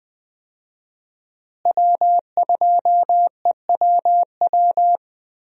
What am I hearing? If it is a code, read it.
Morse code, heard as a keyed tone, about 20 wpm, 700 Hz. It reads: W2EWW
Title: USA Call Signs - 5 Characters - Repeated 2x - 20wpm